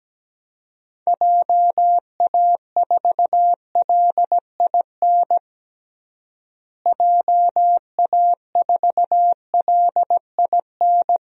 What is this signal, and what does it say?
Morse code, heard as a keyed tone, about 17 wpm, 700 Hz. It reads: JA4LIN JA4LIN
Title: DXCC-100 Call Signs - 6 Characters - Repeated 2x - 17wpm